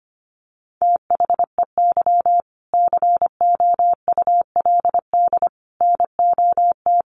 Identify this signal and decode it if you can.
Morse code, heard as a keyed tone, about 25 wpm, 700 Hz. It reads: THEY COULD NOT
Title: Sentences from Top 1100 Words 25wpm